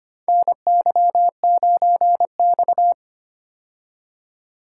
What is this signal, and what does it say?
Morse code, heard as a keyed tone, about 25 wpm, 700 Hz. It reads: NY9X
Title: K1USN SST QSO Practice - 25wpm